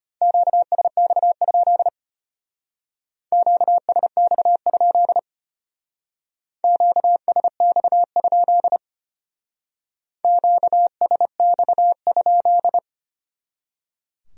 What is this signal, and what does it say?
Morse code, heard as a keyed tone, about 33 wpm, 700 Hz. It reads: QSX? QSX? QSX? QSX?